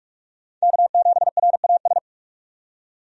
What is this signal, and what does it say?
Morse code, heard as a keyed tone, about 45 wpm, 700 Hz. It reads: K7RAS
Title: USA Call Signs - 5 Characters - 45wpm